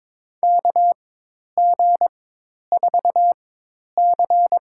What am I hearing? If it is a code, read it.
Morse code, heard as a keyed tone, about 22 wpm, spaced as for 10 wpm, 700 Hz. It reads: KG4C
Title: USA Call Signs - 4x Character Spacing - 22wpm